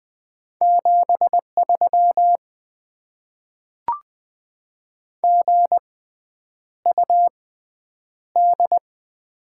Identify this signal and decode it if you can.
Morse code, heard as a keyed tone, about 20 wpm, spaced as for 9 wpm, 700 Hz. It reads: 73 GUD
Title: Top 20 QSO Elements - 6x Character Spacing - 20wpm